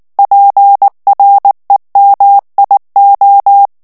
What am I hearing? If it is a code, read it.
Morse code, heard as a keyed tone, about 19 wpm, 800 Hz. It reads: PREMIO
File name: morse mp3 / ITALIEN texte clair 2 vitesse 18 mots minute